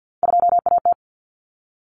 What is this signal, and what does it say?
Morse code, heard as a keyed tone, about 50 wpm, 700 Hz. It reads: 2AT
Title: Common Cut Numbers - 3-Digit 50wpm